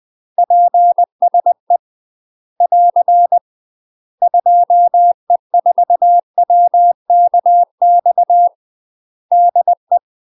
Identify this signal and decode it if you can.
Morse code, heard as a keyed tone, about 20 wpm, 700 Hz. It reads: PSE <AR> 2E4WKX DE
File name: UK MASTER QSOs TRAINING / 20WPM / UK MASTER QSOs 0055